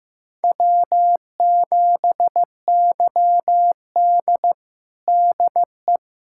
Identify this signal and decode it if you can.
Morse code, heard as a keyed tone, about 15 wpm, 700 Hz. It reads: W7YD DE
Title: USA Callsign DE Callsign - 4 Characters - 15wpm